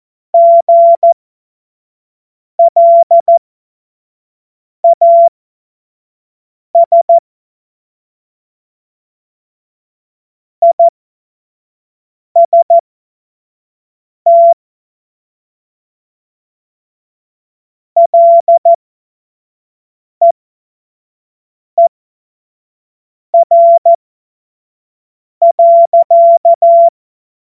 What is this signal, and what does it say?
Morse code, heard as a keyed tone, about 14 wpm, spaced as for 5 wpm, 680 Hz. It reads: GLAS IST LEER.
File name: Morsekurs / Witze / Witz 04. Mai 2021, 25 F70